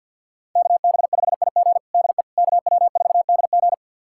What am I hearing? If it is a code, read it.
Morse code, heard as a keyed tone, about 47 wpm, 690 Hz. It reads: KB5IG DE KW4DG
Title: USA Callsign DE Callsign - 5 Characters - 50wpm